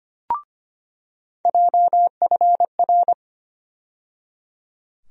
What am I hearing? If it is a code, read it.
Morse code, heard as a keyed tone, about 25 wpm, 700 Hz. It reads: JFR
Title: USA Call Sign Postfixes - 3 Letters 25wpm